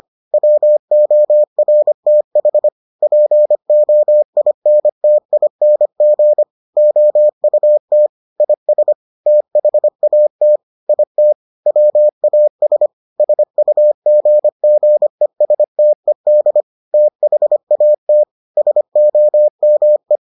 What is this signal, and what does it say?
Morse code, heard as a keyed tone, about 25 wpm, 600 Hz. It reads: WORTH POINTING OUT IS THAT IT WAS SUGGESTED THAT SOME